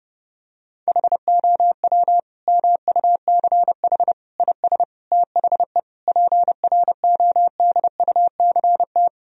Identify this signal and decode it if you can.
Morse code, heard as a keyed tone, about 32 wpm, 700 Hz. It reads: HOW MUCH IS THE PRODUCT
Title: Sentences from Top 1000 Words - Review Part 2 - 30wpm